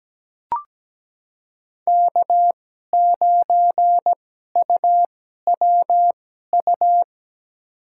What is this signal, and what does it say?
Morse code, heard as a keyed tone, about 17 wpm, spaced as for 12 wpm, 700 Hz. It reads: K9UWU
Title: USA Call Signs - 5 Characters - 2x Character Spacing - 17wpm